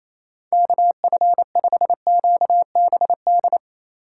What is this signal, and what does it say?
Morse code, heard as a keyed tone, about 28 wpm, 700 Hz. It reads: KF5QBD
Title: USA Call Signs - 6 Characters - 2x Character Spacing - 28wpm